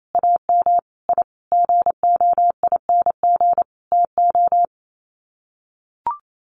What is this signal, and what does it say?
Morse code, heard as a keyed tone, about 28 wpm, 700 Hz. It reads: AM I GOING TO
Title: Sets of 5 Words Encore 28wpm